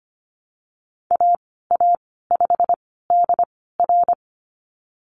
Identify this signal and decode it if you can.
Morse code, heard as a keyed tone, about 25 wpm, spaced as for 16 wpm, 700 Hz. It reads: AA5DR
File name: Lesson 15g - Number 5 - Call Signs 25wpm